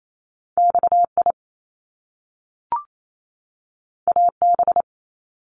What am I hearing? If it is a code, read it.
Morse code, heard as a keyed tone, about 28 wpm, 700 Hz. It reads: XI AB
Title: Top Two Letter Combinations in Words 28wpm